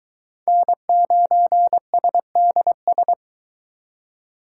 Morse code, heard as a keyed tone, about 23 wpm, 700 Hz. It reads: N9SDS